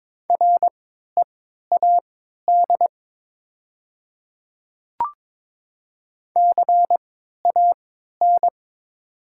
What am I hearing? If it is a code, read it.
Morse code, heard as a keyed tone, about 22 wpm, spaced as for 12 wpm, 700 Hz. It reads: READ CAN
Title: 3x Character Spacing - Top 200 Words - 22wpm